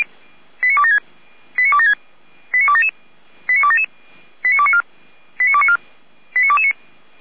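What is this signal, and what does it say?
Five-tone selective call (ZVEI), heard as quick runs of five tones, tones of about 70 ms. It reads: 89276 89276 89277 89277 89294 89294 89229
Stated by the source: ZVEI five-tone selective call received by SDR.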